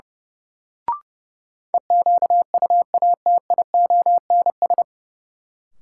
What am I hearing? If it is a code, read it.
Morse code, heard as a keyed tone, about 31 wpm, 700 Hz. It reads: EQUATIONS
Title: Sets of 5 Words - New Word Drill - 30wpm